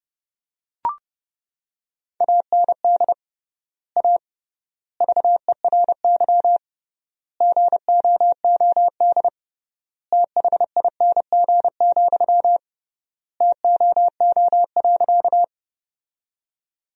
Morse code, heard as a keyed tone, about 32 wpm, 700 Hz. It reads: AND A VERY GOOD THING, TOO.
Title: 3x Word-Spacing - Sentences from Top 200 Words - 30wpm